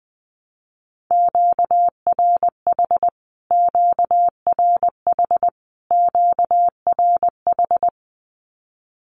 Morse code, heard as a keyed tone, about 20 wpm, 700 Hz. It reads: QRH QRH QRH
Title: Q-Codes Repeated 3x 20wpm